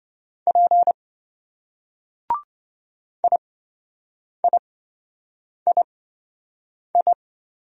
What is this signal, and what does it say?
Morse code, heard as a keyed tone, about 28 wpm, spaced as for 7 wpm, 700 Hz. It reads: P IIII